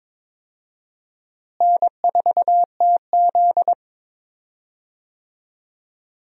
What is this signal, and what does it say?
Morse code, heard as a keyed tone, about 22 wpm, 700 Hz. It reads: N4TZ
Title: USA Call Signs - 4 Characters - Repeated 2x - 22wpm